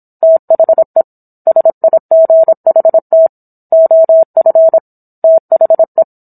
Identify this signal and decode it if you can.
Morse code, heard as a keyed tone, about 27 wpm, 650 Hz. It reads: THE SIGHT OF THE